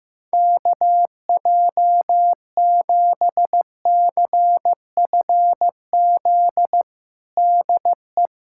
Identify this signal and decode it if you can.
Morse code, heard as a keyed tone, about 16 wpm, 700 Hz. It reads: KJ7CFZ DE
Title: USA Callsign DE Callsign - 6 Characters - 15wpm